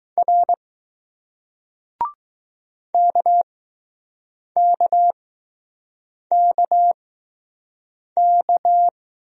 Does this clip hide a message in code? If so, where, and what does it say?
Yes: Morse code from 0.2 to 8.9 s, heard as a keyed tone, about 20 wpm, spaced as for 6 wpm, 700 Hz. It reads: R KKKK